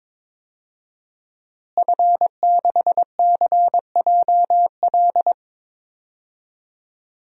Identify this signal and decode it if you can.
Morse code, heard as a keyed tone, about 22 wpm, 700 Hz. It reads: F6CJL
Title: DXCC-100 Call Signs - 5 Characters - Repeated 3x - 22wpm